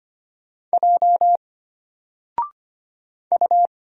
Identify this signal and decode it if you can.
Morse code, heard as a keyed tone, about 25 wpm, 700 Hz. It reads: J U
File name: Single-Letter-Number-25wpm